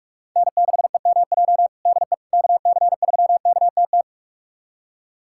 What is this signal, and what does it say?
Morse code, heard as a keyed tone, about 41 wpm, 690 Hz. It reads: N6EMJ DE KC3KTT
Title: USA Callsign DE Callsign 45wpm